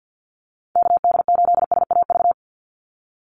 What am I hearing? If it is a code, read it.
Morse code, heard as a keyed tone, about 50 wpm, 700 Hz. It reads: KD8SAV
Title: Call Signs Repeated 2x - 50wpm